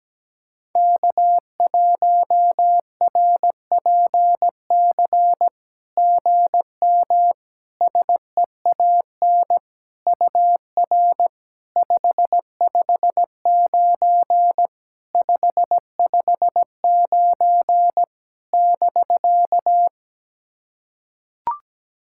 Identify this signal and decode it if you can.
Morse code, heard as a keyed tone, about 17 wpm, 690 Hz. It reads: K1RPC GM SEAN UR 559 559 <BK>
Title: SOTA QSO Practice 17wpm